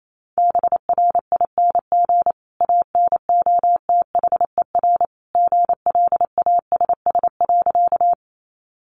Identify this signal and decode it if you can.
Morse code, heard as a keyed tone, about 28 wpm, 700 Hz. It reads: BRING ANOTHER GLASS.
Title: Sentences from Top 1300 Words 28wpm